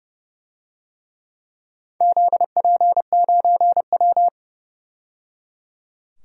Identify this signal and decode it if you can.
Morse code, heard as a keyed tone, about 30 wpm, 700 Hz. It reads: ZP9W
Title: DXCC-100 Call Signs - 4 Characters - 30wpm